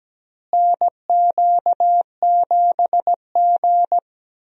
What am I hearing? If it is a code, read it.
Morse code, heard as a keyed tone, about 17 wpm, 700 Hz. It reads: NQ7G
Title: USA Call Signs - 4 Characters - Repeated 3x - 17wpm